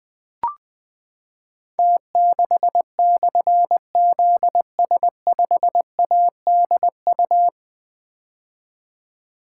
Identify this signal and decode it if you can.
Morse code, heard as a keyed tone, about 20 wpm, 700 Hz. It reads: T6/ZS5ADU
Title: Difficult Call Signs 20wpm